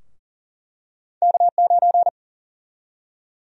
Morse code, heard as a keyed tone, about 40 wpm, 700 Hz. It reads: K9
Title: USA Call Sign Prefixes - 40wpm